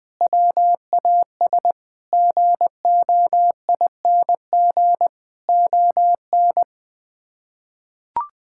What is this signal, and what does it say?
Morse code, heard as a keyed tone, about 20 wpm, 700 Hz. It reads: WAS GOING ON
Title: Sets of 8 Words - 20wpm